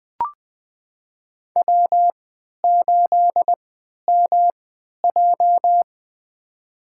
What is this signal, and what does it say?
Morse code, heard as a keyed tone, about 21 wpm, spaced as for 12 wpm, 700 Hz. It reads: W8MJ
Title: USA Call Signs - 4 Characters - 3x Character Spacing - 20wpm